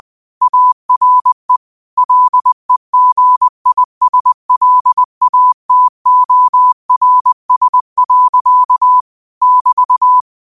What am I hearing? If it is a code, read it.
Morse code, heard as a keyed tone, about 20 wpm, 1000 Hz. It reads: ARE LEGISLATORS. <BT>